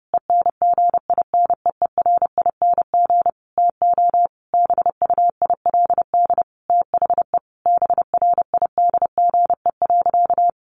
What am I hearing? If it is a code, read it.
Morse code, heard as a keyed tone, about 30 wpm, 700 Hz. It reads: ENGINEERING TO BUILD THE BRIDGE.
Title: Sentences from Top 1500 Words 30wpm